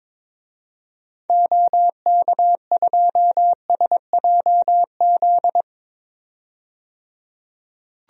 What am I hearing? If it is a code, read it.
Morse code, heard as a keyed tone, about 22 wpm, 700 Hz. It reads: OK2SJZ
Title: DXCC-100 Call Signs - 6 Characters - 22wpm